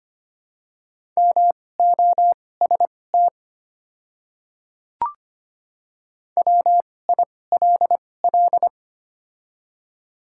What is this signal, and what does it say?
Morse code, heard as a keyed tone, about 25 wpm, spaced as for 18 wpm, 700 Hz. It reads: MOST WILL
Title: 2x Character Spacing - Top 100 Words - 25wpm